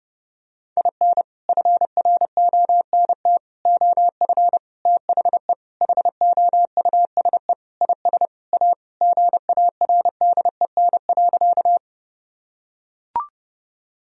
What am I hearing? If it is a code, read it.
Morse code, heard as a keyed tone, about 30 wpm, 700 Hz. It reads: IN FRONT OF THE HOUSE IS A GARDEN.
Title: Sentences from Top 1000 Words - Review Part 4 - 30wpm